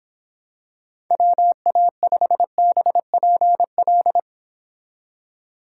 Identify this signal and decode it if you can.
Morse code, heard as a keyed tone, about 27 wpm, 700 Hz. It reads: WA5BPL
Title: Speed Racing - USA Call Signs 22wpm